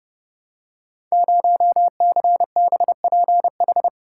Morse code, heard as a keyed tone, about 30 wpm, 700 Hz. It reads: 0CBPH
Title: Code Groups - 5 Letter:Numbers - 30wpm